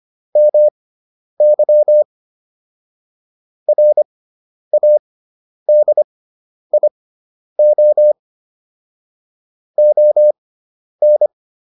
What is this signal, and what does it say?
Morse code, heard as a keyed tone, about 25 wpm, spaced as for 10 wpm, 600 Hz. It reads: MY RADIO ON